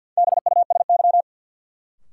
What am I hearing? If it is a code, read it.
Morse code, heard as a keyed tone, about 52 wpm, 700 Hz. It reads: BUSY
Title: Sentences from Top 1500 Words - New Word Drill Alt - 50wpm